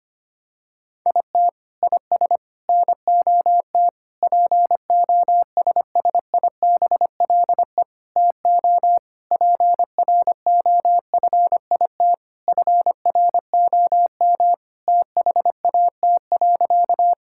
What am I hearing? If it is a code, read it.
Morse code, heard as a keyed tone, about 25 wpm, 700 Hz. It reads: IT IS NOT POSSIBLE TO PROFIT FROM THAT.
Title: Sentences from Top 1000 Words - Review Part 5 - 25wpm